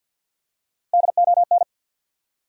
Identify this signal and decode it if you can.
Morse code, heard as a keyed tone, about 52 wpm, 700 Hz. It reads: DON
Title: USA Common Names - Max 4 Letters - 50wpm